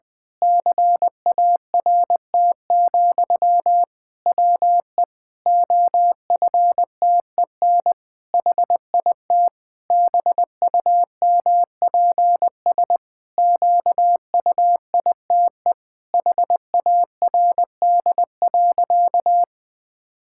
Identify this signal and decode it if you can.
Morse code, heard as a keyed tone, about 20 wpm, 700 Hz. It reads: CART, WE OFTEN HIT BUMPS QUITE HARD.
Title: Pangram Challenge v2 - 20wpm